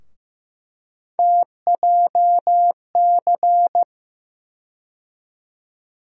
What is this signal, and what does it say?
Morse code, heard as a keyed tone, about 15 wpm, 700 Hz. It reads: TJC
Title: USA Call Sign Postfixes - 3 Letters 15wpm